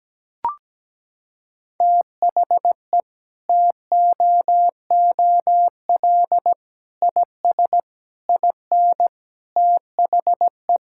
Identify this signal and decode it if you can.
Morse code, heard as a keyed tone, about 17 wpm, 700 Hz. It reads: THE TOOL IS IN THE
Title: Sentences from Top 1000 Words - Review Part 6 - 17wpm